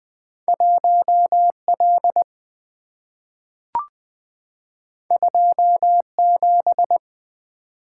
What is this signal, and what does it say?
Morse code, heard as a keyed tone, about 20 wpm, 700 Hz. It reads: 1L 27
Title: Two Letter-Number Combinations - 20wpm